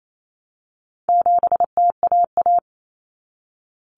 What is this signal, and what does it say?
Morse code, heard as a keyed tone, about 28 wpm, 700 Hz. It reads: 7TAA
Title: Common Cut Numbers - 4-Digit 28wpm